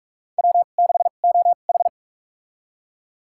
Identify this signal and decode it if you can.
Morse code, heard as a keyed tone, about 42 wpm, spaced as for 31 wpm, 700 Hz. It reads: W6OH
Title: USA Call Signs - 2x Character Spacing - 45wpm